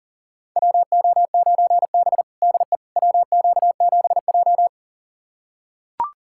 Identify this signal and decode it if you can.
Morse code, heard as a keyed tone, about 40 wpm, 700 Hz. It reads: WO9B DE WQ7J
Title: USA Callsign DE Callsign - 4 Characters - 40wpm